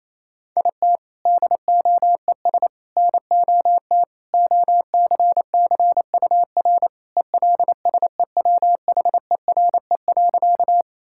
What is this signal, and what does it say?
Morse code, heard as a keyed tone, about 28 wpm, 700 Hz. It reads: IT DOES NOT OCCUR ELSEWHERE.
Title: Sentences from Top 1600 Words 28wpm